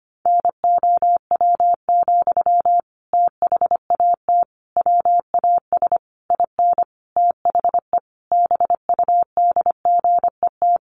Morse code, heard as a keyed tone, about 25 wpm, 700 Hz. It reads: NOW, THAT WAS IN THE BUDGET
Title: Sentences from Top 1200 Words 25wpm